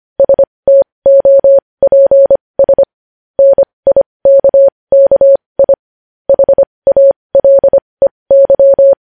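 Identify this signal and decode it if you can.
Morse code, heard as a keyed tone, about 25 wpm, spaced as for 20 wpm, 550 Hz. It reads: STOPS NIKKI HALEY
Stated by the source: Morse code news headlines sent at 20.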